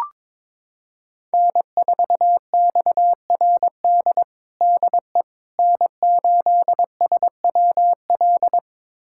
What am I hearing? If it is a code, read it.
Morse code, heard as a keyed tone, about 22 wpm, 700 Hz. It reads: N4XRD DE N8SWL
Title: USA Callsign DE Callsign 22wpm